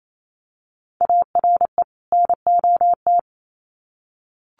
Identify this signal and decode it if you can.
Morse code, heard as a keyed tone, about 28 wpm, 700 Hz. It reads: ARE NOT
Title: Sets of 2 Words 28wpm